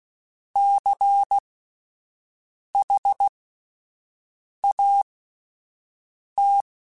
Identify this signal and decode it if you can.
Morse code, heard as a keyed tone, about 16 wpm, spaced as for 5 wpm, 800 Hz. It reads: CHAT
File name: Alphabetical / Alphabetical mp3s / Cs @5